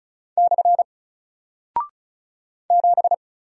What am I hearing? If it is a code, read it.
Morse code, heard as a keyed tone, about 36 wpm, 700 Hz. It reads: / 7
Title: Single Character (Letter:Number:Punctuation:Prosign) 35wpm